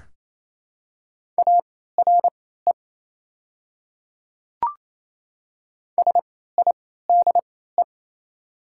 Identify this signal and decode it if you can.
Morse code, heard as a keyed tone, about 28 wpm, spaced as for 16 wpm, 700 Hz. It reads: ARE SIDE
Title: 3x Character Spacing - Top 100 Words - 28wpm